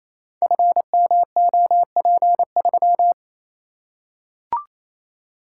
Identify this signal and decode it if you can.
Morse code, heard as a keyed tone, about 28 wpm, 700 Hz. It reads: FMOP3
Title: Code Groups - 5 Letter:Numbers - 28wpm